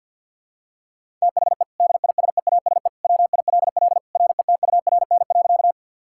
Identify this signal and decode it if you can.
Morse code, heard as a keyed tone, about 50 wpm, 700 Hz. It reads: THE DISEASE WILL RETURN.
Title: Sentences from Top 1000 Words - Review Part 6 - 50wpm